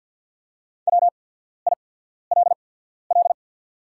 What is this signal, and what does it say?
Morse code, heard as a keyed tone, about 48 wpm, spaced as for 14 wpm, 700 Hz. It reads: WILL
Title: 8x Character Spacing - Top 300 Words - 50wpm